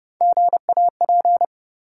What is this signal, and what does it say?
Morse code, heard as a keyed tone, about 30 wpm, 700 Hz. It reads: GAP
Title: Three Letter Words - 30wpm